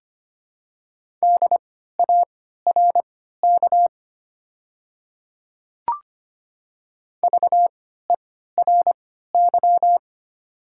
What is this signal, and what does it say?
Morse code, heard as a keyed tone, about 25 wpm, spaced as for 14 wpm, 700 Hz. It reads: DARK VERY
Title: 3x Character Spacing - Top 500 Words - 25wpm